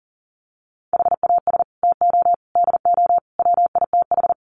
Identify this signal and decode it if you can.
Morse code, heard as a keyed tone, about 40 wpm, 700 Hz. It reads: HAS TO DO WITH
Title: Sets of 4 Words 40wpm